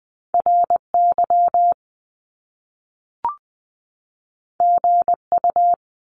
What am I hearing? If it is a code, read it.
Morse code, heard as a keyed tone, about 20 wpm, 700 Hz. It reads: RY GU
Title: Top Two Letter Combinations in Words 20wpm